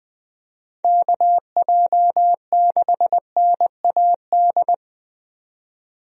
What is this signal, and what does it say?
Morse code, heard as a keyed tone, about 20 wpm, 700 Hz. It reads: KJ6NAD
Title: USA Call Signs - 6 Characters - 20wpm